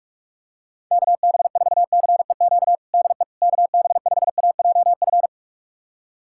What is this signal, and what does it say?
Morse code, heard as a keyed tone, about 46 wpm, 700 Hz. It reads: KB4CEQ DE KB5AJF
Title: USA Callsign DE Callsign 45wpm